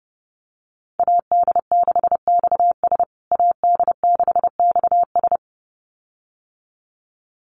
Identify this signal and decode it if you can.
Morse code, heard as a keyed tone, about 30 wpm, 700 Hz. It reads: AD6XS AD6XS
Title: USA Call Signs Repeated 2x 30wpm